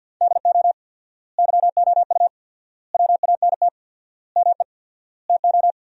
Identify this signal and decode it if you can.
Morse code, heard as a keyed tone, about 50 wpm, 700 Hz. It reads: DO YOU WANT ME TO
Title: 4x Word-Spacing - Sentences from Top 200 Words - 50wpm